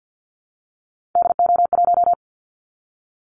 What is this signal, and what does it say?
Morse code, heard as a keyed tone, about 50 wpm, 700 Hz. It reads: DO1
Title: Callsign Prefixes for Top 50 DXCC - 50wpm